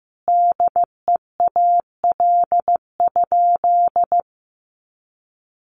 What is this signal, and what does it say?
Morse code, heard as a keyed tone, about 15 wpm, 700 Hz. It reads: DEAL?
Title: Sentences from Top 1400 Words 15wpm